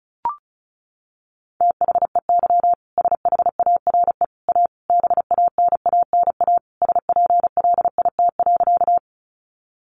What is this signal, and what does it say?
Morse code, heard as a keyed tone, about 34 wpm, 700 Hz. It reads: THEY SHARE A BANANA SPLIT.
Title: Additional Easy Phrases 35wpm